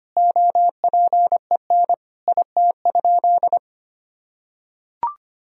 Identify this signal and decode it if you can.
Morse code, heard as a keyed tone, about 25 wpm, 700 Hz. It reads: OPEN IT?
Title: Sentences from Top 1000 Words - Review Part 4 - 25wpm